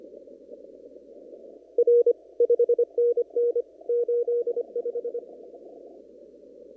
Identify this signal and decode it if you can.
Morse code, heard as a keyed tone, about 25 wpm, 480 Hz. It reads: R 5NN 85
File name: CW / kt5j wpx cw-op-k5pi